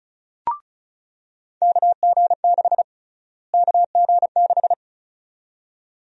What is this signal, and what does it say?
Morse code, heard as a keyed tone, about 36 wpm, 700 Hz. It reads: KG6 KG6
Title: USA Call Sign Prefixes - Repeated 2x - 35wpm